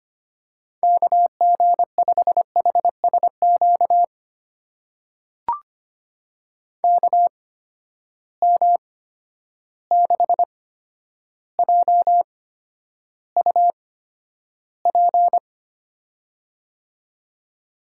Morse code, heard as a keyed tone, about 25 wpm, spaced as for 11 wpm, 700 Hz. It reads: KG5HSQ KM6JUP
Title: USA Call Signs - 6 Characters - 8x Character Spacing - 25wpm